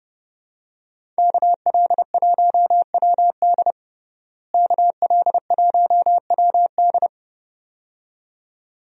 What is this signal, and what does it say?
Morse code, heard as a keyed tone, about 30 wpm, 700 Hz. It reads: KL1WD KL1WD
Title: USA Call Signs - 5 Characters - Repeated 2x - 30wpm